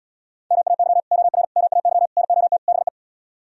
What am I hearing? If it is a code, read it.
Morse code, heard as a keyed tone, about 77 wpm, 700 Hz. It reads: MAY OR MAY NOT BE